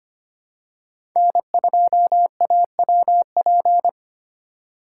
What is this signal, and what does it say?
Morse code, heard as a keyed tone, about 25 wpm, 700 Hz. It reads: N2AWP
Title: USA Call Signs - 5 Characters - 3x Character Spacing - 25wpm